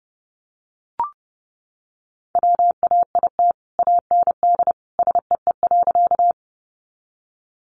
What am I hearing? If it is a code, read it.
Morse code, heard as a keyed tone, about 30 wpm, 700 Hz. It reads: WAIT AND SEE.
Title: Sentences from Top 500 Words 30wpm